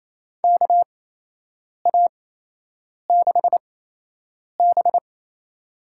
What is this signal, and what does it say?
Morse code, heard as a keyed tone, about 28 wpm, spaced as for 8 wpm, 700 Hz. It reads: KA6B
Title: USA Call Signs - 8x Character Spacing - 28wpm